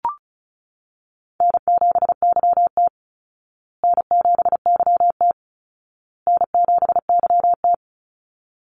Morse code, heard as a keyed tone, about 34 wpm, 700 Hz. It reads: N7YT N7YT N7YT
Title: Easy Call Signs Repeated 3x - 35wpm